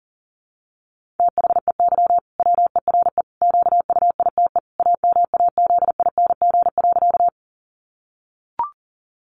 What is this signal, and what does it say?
Morse code, heard as a keyed tone, about 40 wpm, 700 Hz. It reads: THEY WERE QUITE AMAZING.
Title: Sentences from Top 1500 Words 40wpm